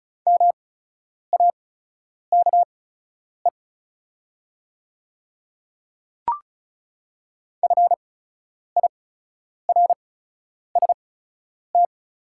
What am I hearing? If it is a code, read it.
Morse code, heard as a keyed tone, about 36 wpm, spaced as for 10 wpm, 690 Hz. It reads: MAKE FIRST
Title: 8x Character Spacing - Top 100 Words - 35wpm